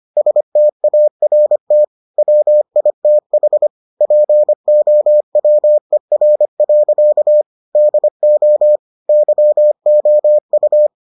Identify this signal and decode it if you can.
Morse code, heard as a keyed tone, about 25 wpm, 600 Hz. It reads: START WITH POWER. DO YOU